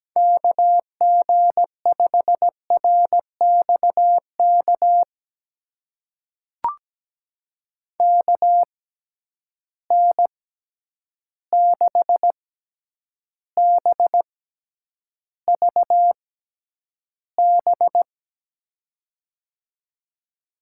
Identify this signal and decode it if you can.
Morse code, heard as a keyed tone, about 17 wpm, spaced as for 9 wpm, 700 Hz. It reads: KG5RXK KN6BVB
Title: USA Call Signs - 6x Character Spacing - 17wpm